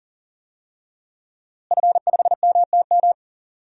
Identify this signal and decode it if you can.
Morse code, heard as a keyed tone, about 40 wpm, 700 Hz. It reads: F5MTM
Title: DXCC-100 Call Signs - 5 Characters - Repeated 3x - 40wpm